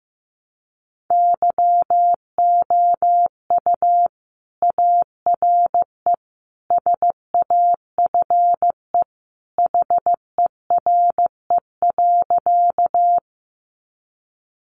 Morse code, heard as a keyed tone, about 15 wpm, 700 Hz. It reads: YOU ARE SAFE HERE.
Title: Sentences from Top 900 Words 15wpm